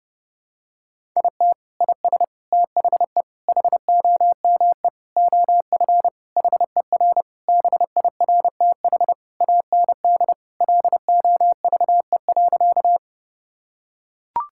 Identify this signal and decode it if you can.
Morse code, heard as a keyed tone, about 32 wpm, 700 Hz. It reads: IT IS THE HOME OF HER BIRTH AND LOVE.
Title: Sentences from Top 1000 Words - Review Part 5 - 30wpm